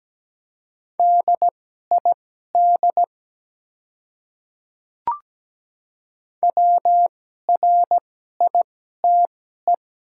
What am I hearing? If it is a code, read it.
Morse code, heard as a keyed tone, about 17 wpm, spaced as for 12 wpm, 700 Hz. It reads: DID WRITE
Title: 2x Character Spacing - Top 100 Words - 17wpm